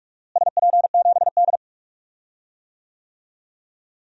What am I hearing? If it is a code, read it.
Morse code, heard as a keyed tone, about 42 wpm, 700 Hz. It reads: SP7D
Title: DXCC-100 Call Signs - 4 Characters - Repeated 2x - 45wpm